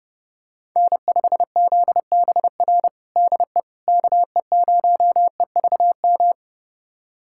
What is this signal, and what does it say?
Morse code, heard as a keyed tone, about 30 wpm, 700 Hz. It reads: N5ZBR DE KE0EVM
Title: USA Callsign DE Callsign 30wpm